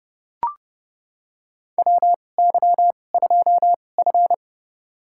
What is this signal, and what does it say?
Morse code, heard as a keyed tone, about 30 wpm, spaced as for 22 wpm, 700 Hz. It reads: WY2F
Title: USA Call Signs - 4 Characters - 2x Character Spacing - 30wpm